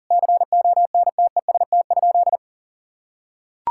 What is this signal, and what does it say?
Morse code, heard as a keyed tone, about 37 wpm, 700 Hz. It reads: CONTEST?
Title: Sentences from Top 1600 Words 40wpm